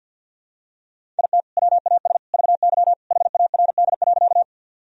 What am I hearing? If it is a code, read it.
Morse code, heard as a keyed tone, about 47 wpm, 700 Hz. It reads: IT WAS VY HARD.
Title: Sentences from Top 300 Words and Abbreviations 50wpm